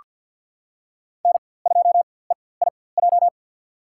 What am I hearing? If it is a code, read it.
Morse code, heard as a keyed tone, about 47 wpm, spaced as for 23 wpm, 700 Hz. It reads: N2EIJ
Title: USA Call Signs - 5 Characters - 4x Character Spacing - 50wpm